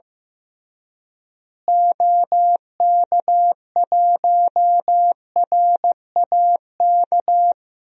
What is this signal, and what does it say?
Morse code, heard as a keyed tone, about 15 wpm, 700 Hz. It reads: OK1RAK
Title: DXCC-100 Call Signs - 6 Characters - Repeated 3x - 15wpm